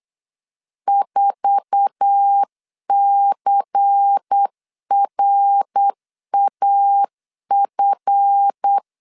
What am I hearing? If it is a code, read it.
Morse code, heard as a keyed tone, about 8 wpm, 800 Hz. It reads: 4CRAF
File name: LEZIONI / 9 286 8wpm 3425